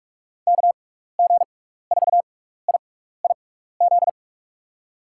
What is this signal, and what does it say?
Morse code, heard as a keyed tone, about 46 wpm, spaced as for 16 wpm, 700 Hz. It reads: KG4IIZ
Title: USA Call Signs - 6 Characters - 6x Character Spacing - 45wpm